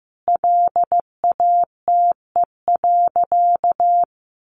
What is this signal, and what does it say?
Morse code, heard as a keyed tone, about 15 wpm, 700 Hz. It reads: LATE.
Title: Sentences from Top 1500 Words 15wpm